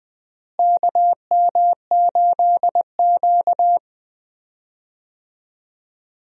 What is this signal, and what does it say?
Morse code, heard as a keyed tone, about 20 wpm, 700 Hz. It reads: KM8Q
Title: USA Call Signs - 4 Characters - Repeated 2x - 20wpm